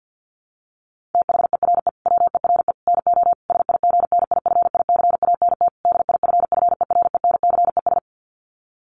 Morse code, heard as a keyed tone, about 48 wpm, 700 Hz. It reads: THERE WERE NO SIGNIFICANT DIFFERENCES
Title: Sets of 5 Words Encore2 50wpm